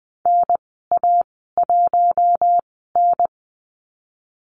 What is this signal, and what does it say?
Morse code, heard as a keyed tone, about 20 wpm, spaced as for 15 wpm, 700 Hz. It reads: NA1N
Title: 048 Lesson 9g - Number 4 - Call Signs 20wpm